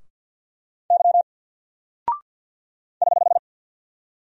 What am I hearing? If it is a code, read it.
Morse code, heard as a keyed tone, about 48 wpm, 700 Hz. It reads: Y <HH>
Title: Single Character (Letter:Number:Punctuation:Prosign) 50wpm